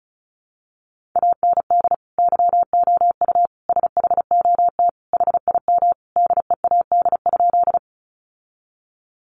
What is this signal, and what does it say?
Morse code, heard as a keyed tone, about 34 wpm, 700 Hz. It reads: AND YOU SHOT HIM DEAD?
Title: Sentences from Top 1500 Words 35wpm